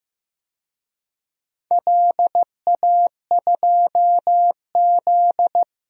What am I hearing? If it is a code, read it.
Morse code, heard as a keyed tone, about 16 wpm, 700 Hz. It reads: LA2Z
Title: DXCC-100 Call Signs - 4 Characters - 15wpm